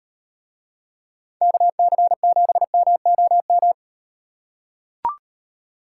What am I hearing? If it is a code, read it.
Morse code, heard as a keyed tone, about 36 wpm, 700 Hz. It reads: KC7MOM